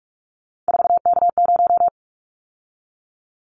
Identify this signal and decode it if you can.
Morse code, heard as a keyed tone, about 46 wpm, 700 Hz. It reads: 4K0
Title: Callsign Prefixes for Top 100 DXCC - 45wpm